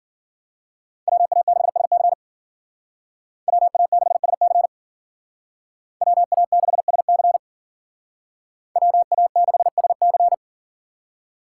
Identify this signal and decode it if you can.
Morse code, heard as a keyed tone, about 52 wpm, 700 Hz. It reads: WA6SC WA6SC WA6SC WA6SC